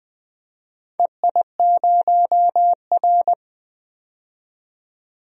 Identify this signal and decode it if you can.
Morse code, heard as a keyed tone, about 20 wpm, 700 Hz. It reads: EI0R
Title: DXCC-100 Call Signs - 4 Characters - 20wpm